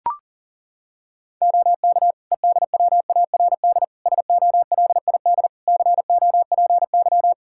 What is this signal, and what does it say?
Morse code, heard as a keyed tone, about 40 wpm, 700 Hz. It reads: OK EDWARD SOLID COPY
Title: Practice QSO - Exchange 3 - More Info - 40wpm